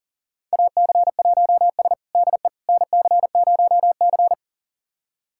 Morse code, heard as a keyed tone, about 40 wpm, 700 Hz. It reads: AC1S DE NC0C
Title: USA Callsign DE Callsign - 4 Characters - 40wpm